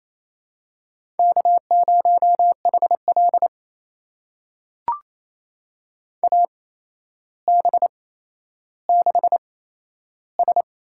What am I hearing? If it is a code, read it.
Morse code, heard as a keyed tone, about 28 wpm, spaced as for 12 wpm, 700 Hz. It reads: K0HL AB6S
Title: USA Call Signs - 4 Characters - 8x Character Spacing - 28wpm